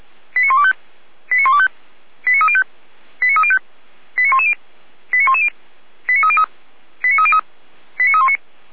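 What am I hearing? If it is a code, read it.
Five-tone selective call (ZVEI), heard as quick runs of five tones, tones of about 70 ms. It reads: 89125 89125 89305 89305 89110 89110 89303 89303 89319
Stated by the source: Five-tone ZVEI selective call received on an SDR.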